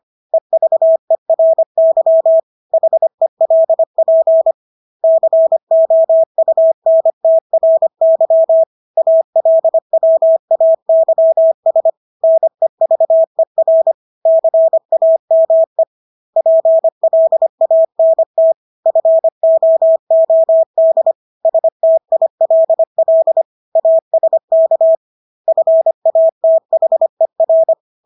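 Morse code, heard as a keyed tone, about 25 wpm, 650 Hz. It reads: EVERY HELP COUNTRY ALWAYS NEVER CAME PLANT FOOD STILL ASK FATHER
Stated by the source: Morse practice of English words at 25 wpm from CW med Gnister.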